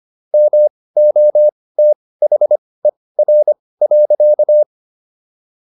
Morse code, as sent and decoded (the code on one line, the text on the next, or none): -- --- - .... .-. .-.-.-
MOTHER.